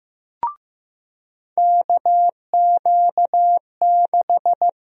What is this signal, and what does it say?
Morse code, heard as a keyed tone, about 15 wpm, 700 Hz. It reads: KQ6
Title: USA Call Sign Prefixes - 15wpm